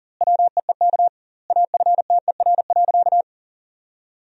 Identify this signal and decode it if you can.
Morse code, heard as a keyed tone, about 40 wpm, 700 Hz. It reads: WEEK AFTER.
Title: 2x Word-Spacing - Sentences from Top 500 Words - 40wpm